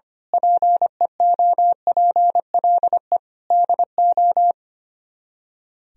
Morse code, heard as a keyed tone, about 24 wpm, 700 Hz. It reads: PEOPLE DO
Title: Speed-Racing - Sets of 5 Words 25wpm